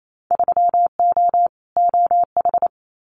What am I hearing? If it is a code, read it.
Morse code, heard as a keyed tone, about 28 wpm, 700 Hz. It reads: 3O OH
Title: Winter Field Day Exchanges 28wpm